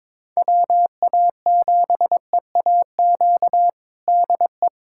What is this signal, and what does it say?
Morse code, heard as a keyed tone, about 22 wpm, 700 Hz. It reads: WA7EAQ DE
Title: USA Callsign DE Callsign 22wpm